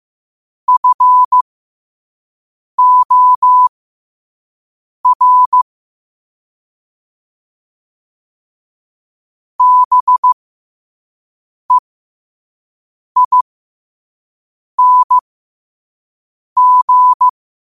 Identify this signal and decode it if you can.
Morse code, heard as a keyed tone, about 15 wpm, spaced as for 5 wpm, 1000 Hz. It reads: FOR BEING
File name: qotdfiles / 5wpm